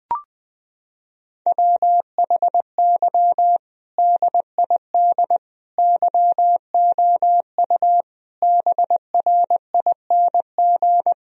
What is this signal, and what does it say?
Morse code, heard as a keyed tone, about 20 wpm, 700 Hz. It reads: WHY DID YOU BRING
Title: Sentences from Top 1000 Words - Review Part 3 - 20wpm